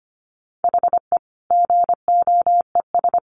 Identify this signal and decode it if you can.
Morse code, heard as a keyed tone, about 25 wpm, 700 Hz. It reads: HE GOES
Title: Additional Easy Phrases 25wpm